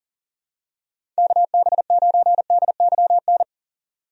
Morse code, heard as a keyed tone, about 40 wpm, 700 Hz. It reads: KB9DYN
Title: USA Call Signs - 6 Characters - 40wpm